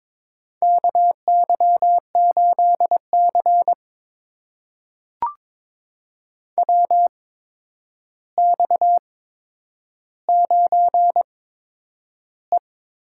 Morse code, heard as a keyed tone, about 22 wpm, spaced as for 9 wpm, 700 Hz. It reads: KY8C WX9E
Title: USA Call Signs - 4 Characters - 8x Character Spacing - 22wpm